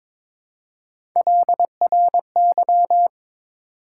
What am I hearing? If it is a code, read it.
Morse code, heard as a keyed tone, about 22 wpm, 700 Hz. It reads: LRY
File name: practice / 22wpm / Three Letter-Number Combinations - 22wpm